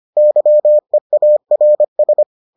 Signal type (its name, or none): Morse code